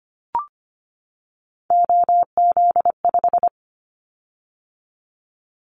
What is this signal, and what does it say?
Morse code, heard as a keyed tone, about 25 wpm, 700 Hz. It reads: OZ5
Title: Callsign Prefixes for Top 50 DXCC - 25wpm